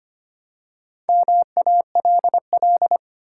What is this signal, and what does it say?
Morse code, heard as a keyed tone, about 25 wpm, 700 Hz. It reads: MALL
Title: Four Letter Words - 25wpm